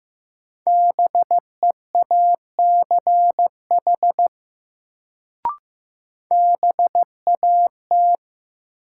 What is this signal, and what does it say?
Morse code, heard as a keyed tone, about 15 wpm, 700 Hz. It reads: BEACH BAT
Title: Sentences from Top 1500 Words - New Word Drill Alt - 15wpm